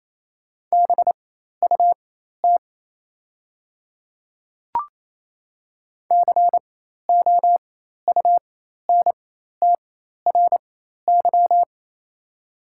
Morse code, heard as a keyed tone, about 28 wpm, spaced as for 13 wpm, 700 Hz. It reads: BUT COUNTRY
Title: 4x Character Spacing - Top 300 Words - 28wpm